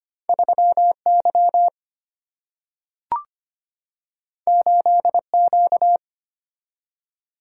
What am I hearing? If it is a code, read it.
Morse code, heard as a keyed tone, about 25 wpm, 700 Hz. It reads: 3 Y 8 Q
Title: Single-Letter Single-Number 25wpm